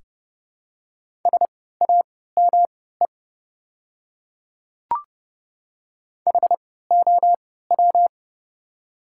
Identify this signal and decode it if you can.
Morse code, heard as a keyed tone, about 30 wpm, spaced as for 17 wpm, 700 Hz. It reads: SAME HOW
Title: 3x Character Spacing - Top 500 Words - 30wpm